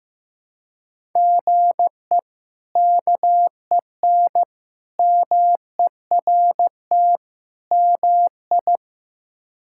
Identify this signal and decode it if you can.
Morse code, heard as a keyed tone, about 15 wpm, 700 Hz. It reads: GE KEN MERT MI
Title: K1USN SST QSO Practice - 15wpm